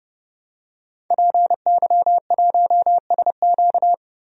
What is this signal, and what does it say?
Morse code, heard as a keyed tone, about 30 wpm, 700 Hz. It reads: PY1SQ
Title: DXCC-100 Call Signs - 5 Characters - 30wpm